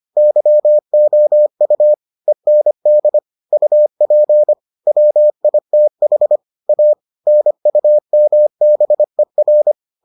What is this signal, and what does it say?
Morse code, heard as a keyed tone, about 25 wpm, 600 Hz. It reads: YOU END UP WITH A NUMBER